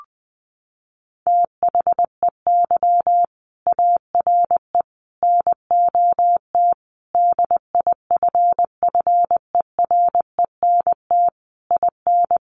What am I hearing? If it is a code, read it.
Morse code, heard as a keyed tone, about 20 wpm, 700 Hz. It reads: THEY ARE NOT DIFFERENT IN
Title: Sentences from Top 1400 Words 20wpm